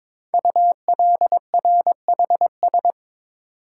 Morse code, heard as a keyed tone, about 22 wpm, 700 Hz. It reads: ULRHS
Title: Code Groups - 5 Letters - 22wpm